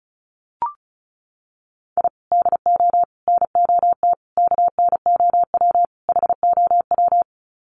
Morse code, heard as a keyed tone, about 34 wpm, 700 Hz. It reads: I DO NOT KNOW HOW
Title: Sets of 5 Words 35wpm